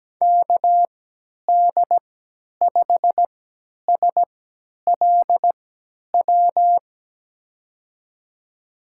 Morse code, heard as a keyed tone, about 17 wpm, spaced as for 10 wpm, 700 Hz. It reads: KD5SLW
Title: USA Call Signs - 6 Characters - 3x Character Spacing - 17wpm